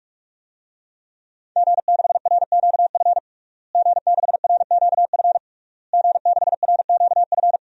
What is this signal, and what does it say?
Morse code, heard as a keyed tone, about 46 wpm, 700 Hz. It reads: G6RQF G6RQF G6RQF
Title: DXCC-100 Call Signs - 5 Characters - Repeated 3x - 45wpm